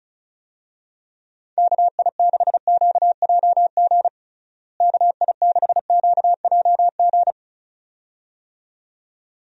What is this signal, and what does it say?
Morse code, heard as a keyed tone, about 36 wpm, 700 Hz. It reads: KI6QJG KI6QJG
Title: USA Call Signs - 6 Characters - Repeated 2x - 35wpm